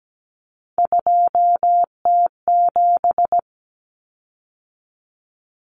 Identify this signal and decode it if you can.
Morse code, heard as a keyed tone, about 17 wpm, 690 Hz. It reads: 2T7
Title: Common Cut Numbers - 3-Digit 17wpm